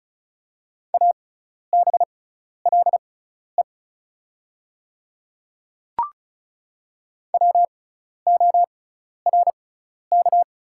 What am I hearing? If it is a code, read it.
Morse code, heard as a keyed tone, about 34 wpm, spaced as for 12 wpm, 690 Hz. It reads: ABLE WORK
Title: 6x Character Spacing - Top 500 Words - 35wpm